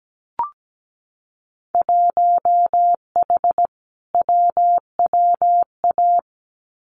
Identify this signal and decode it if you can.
Morse code, heard as a keyed tone, about 17 wpm, 700 Hz. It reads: 1H WWA
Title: Winter Field Day Exchanges 17wpm